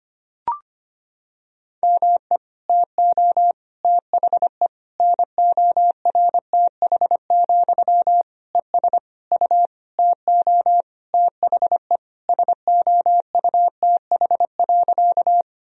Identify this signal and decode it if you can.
Morse code, heard as a keyed tone, about 25 wpm, 700 Hz. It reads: ME TO THE NORTH, ES U TO THE SOUTH.
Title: Sentences from Top 400 Words and Abbreviations 25wpm